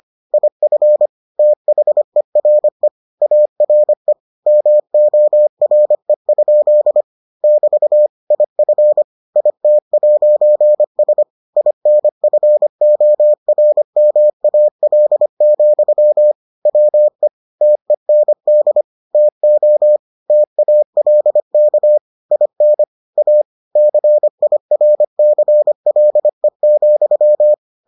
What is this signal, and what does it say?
Morse code, heard as a keyed tone, about 25 wpm, 600 Hz. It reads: IF THERE ARE MORE? <BT> IF IT'S INFORMAL, WE TEND TO TALK IN A CIRCLE,